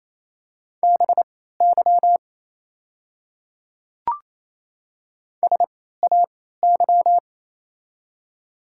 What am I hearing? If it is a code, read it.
Morse code, heard as a keyed tone, about 28 wpm, spaced as for 16 wpm, 700 Hz. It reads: BY SAY
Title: 3x Character Spacing - Top 300 Words - 28wpm